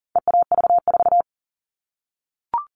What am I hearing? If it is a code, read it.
Morse code, heard as a keyed tone, about 40 wpm, 700 Hz. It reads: EAV4
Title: All Cut Numbers - 4-Digit 40wpm